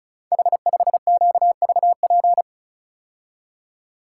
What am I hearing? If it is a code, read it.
Morse code, heard as a keyed tone, about 36 wpm, 700 Hz. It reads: H5QVP
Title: Code Groups - 5 Letter:Numbers - 35wpm